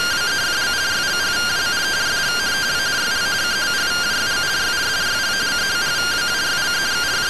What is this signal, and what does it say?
Two-tone teleprinter signal, about 170 Hz shift.